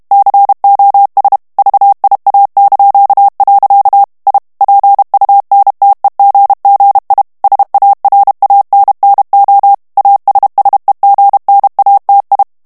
Morse code, heard as a keyed tone, about 32 wpm, 800 Hz. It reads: COS VIA). I PUNTEGGI SARANNO ASSEGNATI
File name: ITALIEN texte clair 2 vitesse 30 mots minute